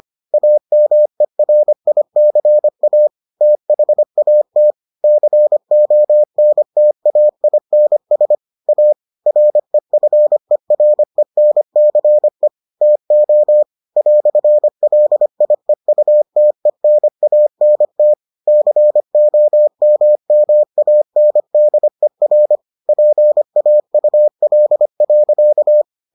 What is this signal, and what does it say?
Morse code, heard as a keyed tone, about 25 wpm, 600 Hz. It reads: AMERICA THAT CONTAINS A REFERENCE TO "LIEUTENANT COMMANDER PAUL.